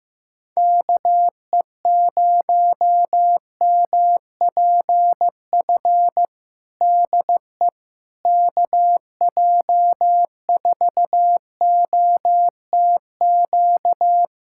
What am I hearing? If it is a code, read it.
Morse code, heard as a keyed tone, about 15 wpm, 700 Hz. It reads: KE0MPF DE KJ4OTQ
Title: Practice QSO - Exchange 4 - Ending - 15wpm